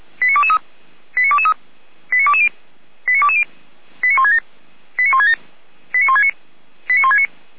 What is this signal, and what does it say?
Five-tone selective call (ZVEI), heard as quick runs of five tones, tones of about 70 ms. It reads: 89223 89223 89220 89220 89167 89167 89169 89169